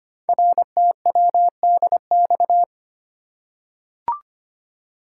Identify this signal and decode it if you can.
Morse code, heard as a keyed tone, about 25 wpm, 700 Hz. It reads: RTWDX
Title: Code Groups - 5 Letters - 25wpm